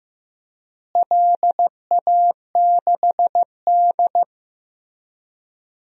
Morse code, heard as a keyed tone, about 15 wpm, 700 Hz. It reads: LA6D
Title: DXCC-100 Call Signs - 4 Characters - Repeated 2x - 15wpm